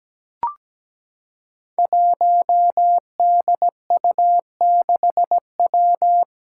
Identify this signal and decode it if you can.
Morse code, heard as a keyed tone, about 17 wpm, 700 Hz. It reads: 1DU6W
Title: Five Challenging Letter-Number Combinations 17wpm